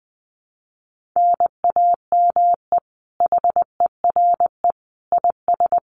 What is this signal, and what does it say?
Morse code, heard as a keyed tone, about 20 wpm, 700 Hz. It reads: NAME HERE IS
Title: Common QSO Phrases 20wpm